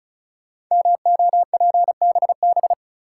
Morse code, heard as a keyed tone, about 36 wpm, 700 Hz. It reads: MOPBB
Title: Code Groups - 5 Letters - 35wpm